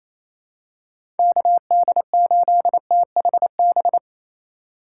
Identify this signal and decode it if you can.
Morse code, heard as a keyed tone, about 28 wpm, 700 Hz. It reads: KD8THB
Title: USA Call Signs - 6 Characters - 2x Character Spacing - 28wpm